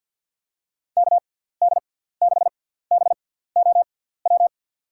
Morse code, heard as a keyed tone, about 48 wpm, spaced as for 17 wpm, 700 Hz. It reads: KD6BOW
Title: USA Call Signs - 6 Characters - 6x Character Spacing - 50wpm